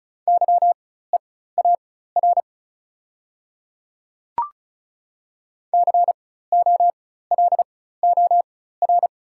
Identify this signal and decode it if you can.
Morse code, heard as a keyed tone, about 36 wpm, spaced as for 16 wpm, 690 Hz. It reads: YEAR COLOR
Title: 4x Character Spacing - Top 300 Words - 35wpm